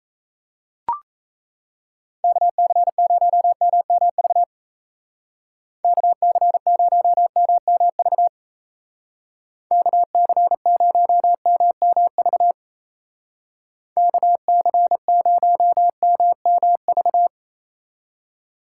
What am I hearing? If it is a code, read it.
Morse code, heard as a keyed tone, about 35 wpm, 700 Hz. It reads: KC0MMV KC0MMV KC0MMV KC0MMV
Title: Speed Racing - USA 2x3 Call Signs 28wpm